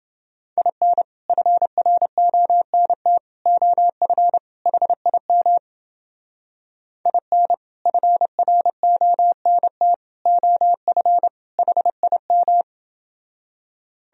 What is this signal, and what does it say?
Morse code, heard as a keyed tone, about 28 wpm, 700 Hz. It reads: IN FRONT OF HIM IN FRONT OF HIM